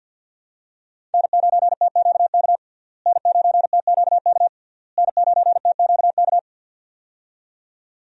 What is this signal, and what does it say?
Morse code, heard as a keyed tone, about 48 wpm, 700 Hz. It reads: N9TQK N9TQK N9TQK
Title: USA Call Signs - 5 Characters - Repeated 3x - 50wpm